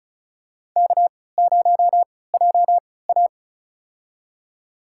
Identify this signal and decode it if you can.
Morse code, heard as a keyed tone, about 36 wpm, spaced as for 20 wpm, 700 Hz. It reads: K0JA
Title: USA Call Signs - 3x Character Spacing - 35wpm